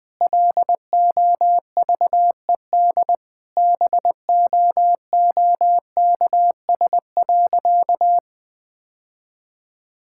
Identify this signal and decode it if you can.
Morse code, heard as a keyed tone, about 20 wpm, 700 Hz. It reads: LOVED BOOKS.
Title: Sentences with 3-Word Phrases - 20wpm